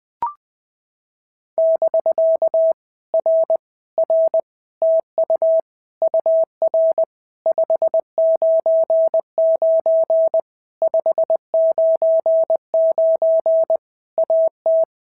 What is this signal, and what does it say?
Morse code, heard as a keyed tone, about 19 wpm, 650 Hz. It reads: <BK> R R TU UR 599 599 AT
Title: POTA QSO Practice - 20wpm